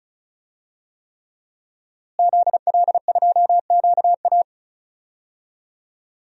Morse code, heard as a keyed tone, about 36 wpm, 700 Hz. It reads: ZL2QA
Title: DXCC-100 Call Signs - 5 Characters - Repeated 3x - 35wpm